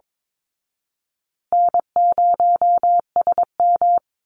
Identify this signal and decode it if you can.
Morse code, heard as a keyed tone, about 22 wpm, 700 Hz. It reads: N0SM
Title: Call Signs Repeated 2x - 22wpm